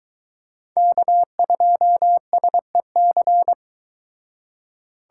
Morse code, heard as a keyed tone, about 23 wpm, 700 Hz. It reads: K2SEC